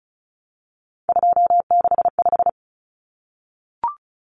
Morse code, heard as a keyed tone, about 34 wpm, 700 Hz. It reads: 265
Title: Numbers - 3-Digit 35wpm